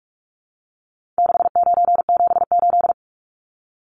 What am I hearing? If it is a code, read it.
Morse code, heard as a keyed tone, about 44 wpm, 700 Hz. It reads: 6978
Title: Numbers - 4-Digit 45wpm